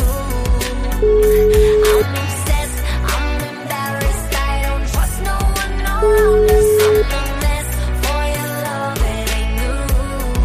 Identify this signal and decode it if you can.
Ringback tone (Europe, 425 Hz), with noise.